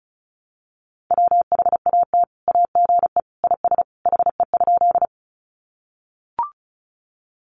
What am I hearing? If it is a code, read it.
Morse code, heard as a keyed tone, about 34 wpm, 700 Hz. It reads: WHAT AGE IS HE?
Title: Sentences from Top 500 Words 35wpm